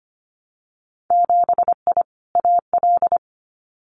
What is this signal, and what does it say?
Morse code, heard as a keyed tone, about 25 wpm, 700 Hz. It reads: 7I AL
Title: Winter Field Day Exchanges Encore 25wpm